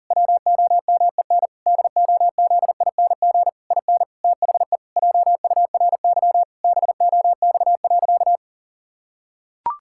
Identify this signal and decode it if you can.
Morse code, heard as a keyed tone, about 40 wpm, 700 Hz. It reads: WOMEN DOZING IN THE JURY BOX.
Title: Pangram Challenge v2 - 40wpm